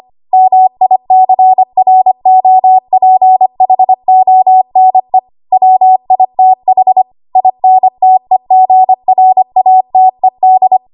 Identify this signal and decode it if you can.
Morse code, heard as a keyed tone, about 25 wpm, 750 Hz. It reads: MICROPHONE WITH INTEGRATED
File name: Archive / 25 WPM / 25WPM